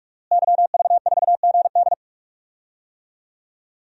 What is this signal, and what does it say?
Morse code, heard as a keyed tone, about 46 wpm, 700 Hz. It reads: YV4GD
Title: DXCC-100 Call Signs - 5 Characters - Repeated 2x - 45wpm